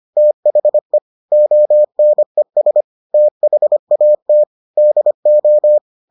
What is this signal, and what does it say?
Morse code, heard as a keyed tone, about 25 wpm, 600 Hz. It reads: THE ONES THAT DO